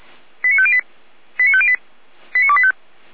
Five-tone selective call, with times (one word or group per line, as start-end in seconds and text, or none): 0.4-0.8 89508
1.4-1.8 89508
2.3-2.7 89285